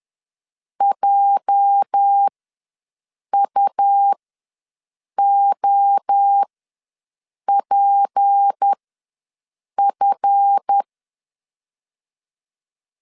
Morse code, heard as a keyed tone, about 10 wpm, spaced as for 6 wpm, 800 Hz. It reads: JUOPF
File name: LEZIONI / 8 YJ45 / 8 YJ ws 10wpm 4043